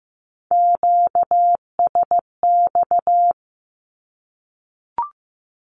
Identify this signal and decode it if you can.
Morse code, heard as a keyed tone, about 15 wpm, 700 Hz. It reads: QSX
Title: Q-Codes Repeated 3x 15wpm